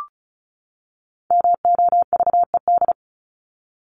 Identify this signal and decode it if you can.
Morse code, heard as a keyed tone, about 34 wpm, 700 Hz. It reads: MOVED
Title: ED Suffix - Top 100 Words with Base Words 35wpm